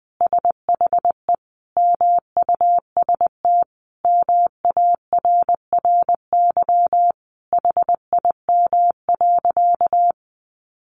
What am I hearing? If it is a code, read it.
Morse code, heard as a keyed tone, about 20 wpm, 700 Hz. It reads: SHE MUST MARRY HIM.
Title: Sentences from Top 1200 Words 20wpm